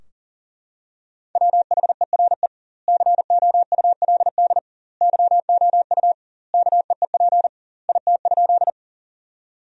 Morse code, heard as a keyed tone, about 40 wpm, 700 Hz. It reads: WHERE COULD YOU KEEP IT?
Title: 2x Word-Spacing - Sentences from Top 300 Words - 40wpm